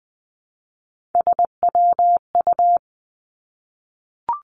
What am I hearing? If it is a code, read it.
Morse code, heard as a keyed tone, about 20 wpm, 700 Hz. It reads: SWU
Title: Three Challenging Letter-Number Combinations 20wpm